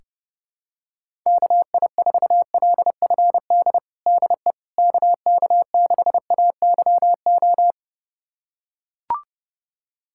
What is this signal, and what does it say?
Morse code, heard as a keyed tone, about 31 wpm, 700 Hz. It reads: KI4LFD DE KK6AYO
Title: USA Callsign DE Callsign 30wpm